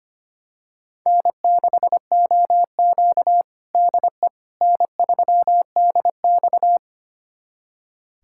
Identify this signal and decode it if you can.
Morse code, heard as a keyed tone, about 25 wpm, 700 Hz. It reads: N6OQ DE N3DX
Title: USA Callsign DE Callsign - 4 Characters - 25wpm